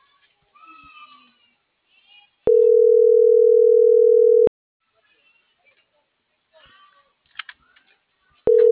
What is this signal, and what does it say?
Ringback tone (North America), 440 + 480 Hz.